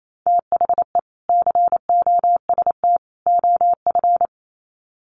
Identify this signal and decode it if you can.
Morse code, heard as a keyed tone, about 28 wpm, 700 Hz. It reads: THE COST OF
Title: Sets of 3 Words 28wpm